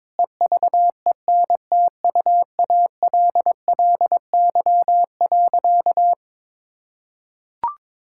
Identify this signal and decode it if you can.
Morse code, heard as a keyed tone, about 22 wpm, 700 Hz. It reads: EVENTUALLY.
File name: practice / 22wpm / Sentences with 8-Word Phrases - 22wpm